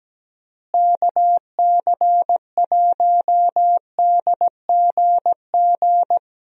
Morse code, heard as a keyed tone, about 17 wpm, 700 Hz. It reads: KC1DGG
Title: USA Call Signs - 6 Characters - Repeated 2x - 17wpm